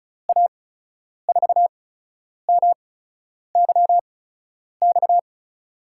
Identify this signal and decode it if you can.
Morse code, heard as a keyed tone, about 36 wpm, spaced as for 10 wpm, 700 Hz. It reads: A4MYX
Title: USA Call Signs - 8x Character Spacing - 35wpm